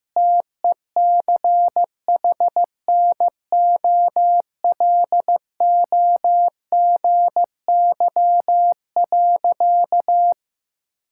Morse code, heard as a keyed tone, about 15 wpm, 700 Hz. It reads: TECHNOLOGY.
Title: Sentences from Top 1000 Words - Review Part 5 - 15wpm